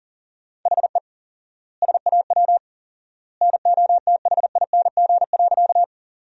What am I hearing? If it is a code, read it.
Morse code, heard as a keyed tone, about 37 wpm, 700 Hz. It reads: HE SAW NOTHING.
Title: 4x Word-Spacing - Sentences from Top 500 Words - 40wpm